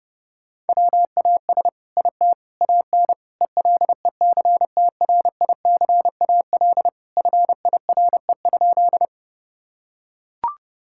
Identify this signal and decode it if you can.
Morse code, heard as a keyed tone, about 32 wpm, 700 Hz. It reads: WAS IT AN ELECTRICAL FIRE?
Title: Sentences from Top 1000 Words - Review Part 6 - 30wpm